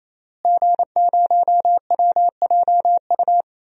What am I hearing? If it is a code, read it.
Morse code, heard as a keyed tone, about 28 wpm, 700 Hz. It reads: G0WJU
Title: DXCC-100 Call Signs - 5 Characters - Repeated 3x - 28wpm